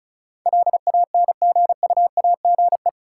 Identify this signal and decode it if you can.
Morse code, heard as a keyed tone, about 34 wpm, 700 Hz. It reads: LANGUAGE
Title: Sentences from Top 500 Words - New Word Drill - 35wpm